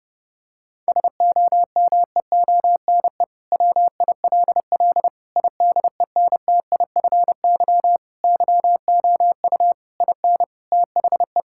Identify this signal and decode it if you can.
Morse code, heard as a keyed tone, about 29 wpm, 700 Hz. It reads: SOMEONE WILL IDENTIFY YOU IN THE
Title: Sentences from Top 1000 Words - Review Part 6 - 30wpm